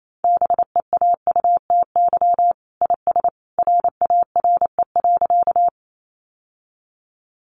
Morse code, heard as a keyed tone, about 28 wpm, 700 Hz. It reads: BEAUTY IS RARE.
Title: Sentences from Top 1100 Words 28wpm